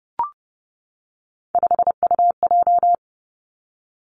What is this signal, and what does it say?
Morse code, heard as a keyed tone, about 30 wpm, 700 Hz. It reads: 5UJ
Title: Three Challenging Letter-Number Combinations 30wpm